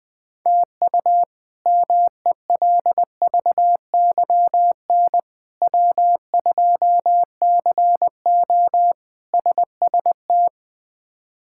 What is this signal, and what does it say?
Morse code, heard as a keyed tone, about 21 wpm, 700 Hz. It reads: TU MELVYN W2CO SST
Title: K1USN SST QSO Practice - 20wpm